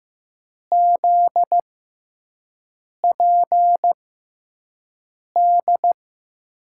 Morse code, heard as a keyed tone, about 15 wpm, spaced as for 5 wpm, 700 Hz. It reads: ZPD
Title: USA Call Signs - 6 Characters - 6x Character Spacing - 15wpm